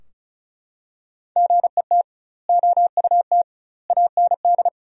Morse code, heard as a keyed tone, about 34 wpm, 700 Hz. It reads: GET OUT AND
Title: 2x Word-Spacing - Sentences from Top 200 Words - 35wpm